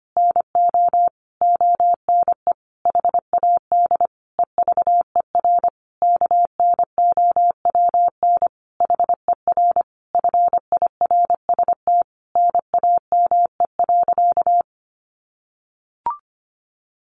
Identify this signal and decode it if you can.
Morse code, heard as a keyed tone, about 25 wpm, 700 Hz. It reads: NO ONE HAD EVER KNOWN HER FIRST NAME.
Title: Sentences from Top 1200 Words 25wpm